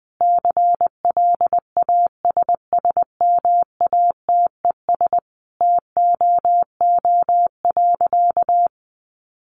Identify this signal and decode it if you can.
Morse code, heard as a keyed tone, about 20 wpm, 700 Hz. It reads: CLASSMATES TOO.
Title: Additional Easy Phrases 20wpm